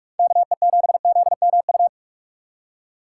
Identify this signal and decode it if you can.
Morse code, heard as a keyed tone, about 46 wpm, 700 Hz. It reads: KE7ZMU
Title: USA Call Signs - 6 Characters - 2x Character Spacing - 45wpm